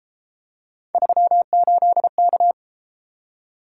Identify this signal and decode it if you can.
Morse code, heard as a keyed tone, about 32 wpm, 700 Hz. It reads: 38K